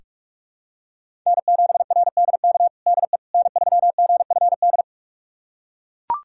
Morse code, heard as a keyed tone, about 42 wpm, 700 Hz. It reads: N7RDK DE N3GFD
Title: USA Callsign DE Callsign - 5 Characters - 45wpm